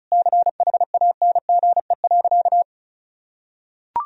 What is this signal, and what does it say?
Morse code, heard as a keyed tone, about 36 wpm, 700 Hz. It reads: CHANGE.
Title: Sentences from Top 1000 Words - Review Part 1 - 35wpm